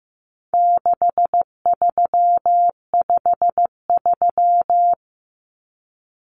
Morse code, heard as a keyed tone, about 15 wpm, 700 Hz. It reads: 6353
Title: Numbers - 4-Digit 15wpm